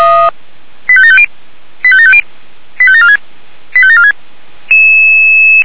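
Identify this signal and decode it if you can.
Five-tone selective call (ZVEI), heard as quick runs of five tones, tones of about 70 ms. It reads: 85750 85750 85746 85746
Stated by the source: ZVEI five-tone call 85827, received on SDR.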